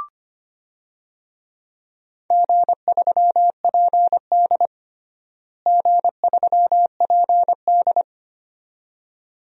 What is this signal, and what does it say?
Morse code, heard as a keyed tone, about 25 wpm, 700 Hz. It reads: G3PD G3PD
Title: DXCC-100 Call Signs - 4 Characters - Repeated 2x - 25wpm